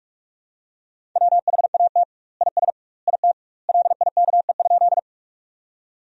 Morse code, heard as a keyed tone, about 46 wpm, 700 Hz. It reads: WHAT IS IT LIKE?
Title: 2x Word-Spacing - Sentences from Top 100 Words - 45wpm